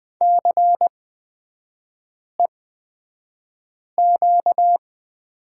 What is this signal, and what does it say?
Morse code, heard as a keyed tone, about 20 wpm, spaced as for 5 wpm, 700 Hz. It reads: CEQ
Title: 250 Lesson 40e - BK - Three Characters 20wpm